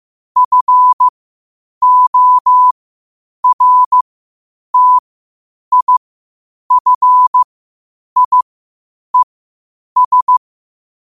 Morse code, heard as a keyed tone, about 15 wpm, spaced as for 8 wpm, 1000 Hz. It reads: FORTIFIES
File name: qotdfiles / 8wpm